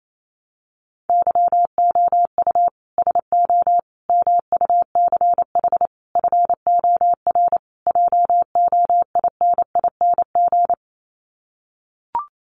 Morse code, heard as a keyed tone, about 28 wpm, 700 Hz. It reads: YOU SO MUCH FOR JOINING
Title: Sets of 5 Words Encore2 28wpm